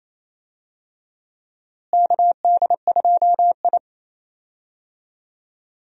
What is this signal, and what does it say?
Morse code, heard as a keyed tone, about 28 wpm, 700 Hz. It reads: KD2I
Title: USA Call Signs - 4 Characters - Repeated 2x - 28wpm